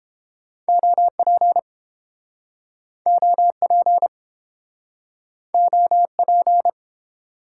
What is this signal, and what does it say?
Morse code, heard as a keyed tone, about 32 wpm, 700 Hz. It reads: OP OP OP